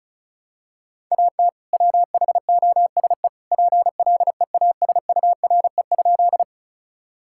Morse code, heard as a keyed tone, about 34 wpm, 700 Hz. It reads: AT WHOSE PLEASURE?
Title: Sentences from Top 1600 Words 35wpm